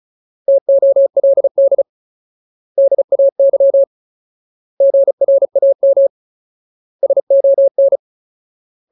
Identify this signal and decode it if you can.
Morse code, heard as a keyed tone, about 34 wpm, 550 Hz. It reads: TOLD DAY GRAM SON
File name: Ultimate Word List / 35WPM Ultimate